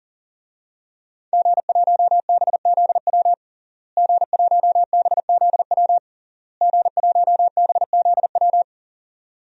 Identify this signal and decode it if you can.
Morse code, heard as a keyed tone, about 40 wpm, 700 Hz. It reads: G1BZW G1BZW G1BZW
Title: DXCC-100 Call Signs - 5 Characters - Repeated 3x - 40wpm